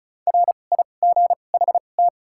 Morse code, heard as a keyed tone, about 36 wpm, spaced as for 25 wpm, 700 Hz. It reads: RIGHT
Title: 2x Character Spacing - Top 200 Words - 35wpm